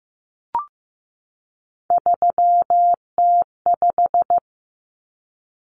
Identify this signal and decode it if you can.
Morse code, heard as a keyed tone, about 15 wpm, 700 Hz. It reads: 3T5
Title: Common Cut Numbers - 3-Digit 15wpm